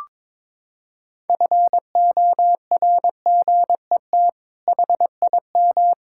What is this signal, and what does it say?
Morse code, heard as a keyed tone, about 22 wpm, 700 Hz. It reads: FORGET HIM
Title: Sentences from Top 1600 Words 22wpm